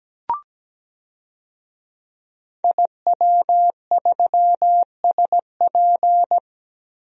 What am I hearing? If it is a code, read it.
Morse code, heard as a keyed tone, about 17 wpm, 700 Hz. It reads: IW3SP
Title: DXCC-100 Call Signs - 5 Characters - Repeated 2x - 17wpm